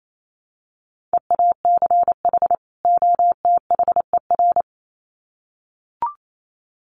Morse code, heard as a keyed tone, about 28 wpm, 700 Hz. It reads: EACH OTHER
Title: Sets of 2 Words 28wpm